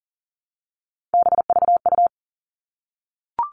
Morse code, heard as a keyed tone, about 40 wpm, 700 Hz. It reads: BVU
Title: All Cut Numbers - 3-Digit 40wpm